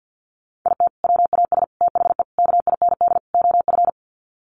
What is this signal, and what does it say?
Morse code, heard as a keyed tone, about 50 wpm, 700 Hz. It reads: IT WAS THE KIND OF
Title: Sets of 5 Words Encore2 50wpm